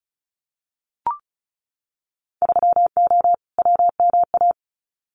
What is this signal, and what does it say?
Morse code, heard as a keyed tone, about 34 wpm, 700 Hz. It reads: 3O WMA
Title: Winter Field Day Exchanges Encore 35wpm